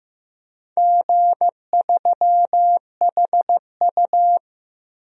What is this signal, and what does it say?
Morse code, heard as a keyed tone, about 15 wpm, 700 Hz. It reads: G3HU
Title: DXCC-100 Call Signs - 4 Characters - 15wpm